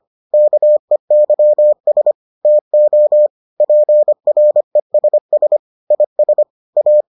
Morse code, heard as a keyed tone, about 25 wpm, 600 Hz. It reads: KEYS TO PRESS IS A